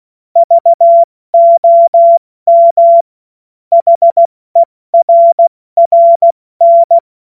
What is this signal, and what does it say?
Morse code, heard as a keyed tone, about 16 wpm, 680 Hz. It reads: VOM HERRN
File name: Morsekurs / Witze / Witz 13 Februar 2024 70 F80